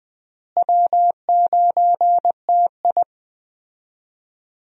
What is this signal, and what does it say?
Morse code, heard as a keyed tone, about 20 wpm, 700 Hz. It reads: W9TI
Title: USA Call Signs - 4 Characters - Repeated 3x - 20wpm